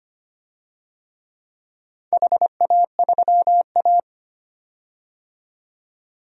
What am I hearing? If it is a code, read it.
Morse code, heard as a keyed tone, about 25 wpm, 700 Hz. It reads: HA3A
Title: DXCC-100 Call Signs - 4 Characters - 25wpm